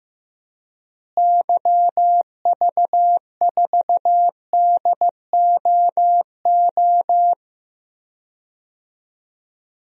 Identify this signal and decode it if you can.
Morse code, heard as a keyed tone, about 16 wpm, 700 Hz. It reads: YV4DOO
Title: DXCC-100 Call Signs - 6 Characters - Repeated 3x - 15wpm